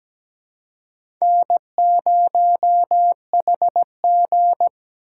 Morse code, heard as a keyed tone, about 17 wpm, 700 Hz. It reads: N0HG
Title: USA Call Signs - 4 Characters - 4x Character Spacing - 17wpm